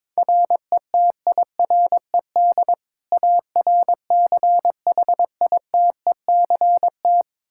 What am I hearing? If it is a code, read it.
Morse code, heard as a keyed tone, about 21 wpm, 700 Hz. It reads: RETIRED ARCHITECT
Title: Practice QSO - Exchange 3 - More Info - 22wpm